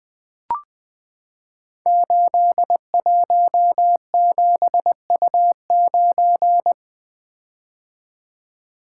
Morse code, heard as a keyed tone, about 20 wpm, 700 Hz. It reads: 817U9
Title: Code Groups - 5 Letter:Numbers - 20wpm